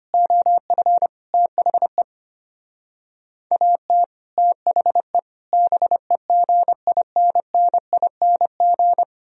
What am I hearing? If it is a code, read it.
Morse code, heard as a keyed tone, about 32 wpm, 700 Hz. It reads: OF THE AT THE BEGINNING